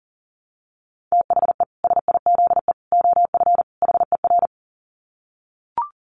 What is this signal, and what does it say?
Morse code, heard as a keyed tone, about 40 wpm, 700 Hz. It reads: THE SIZE OF HER
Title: Dit Challenge 40wpm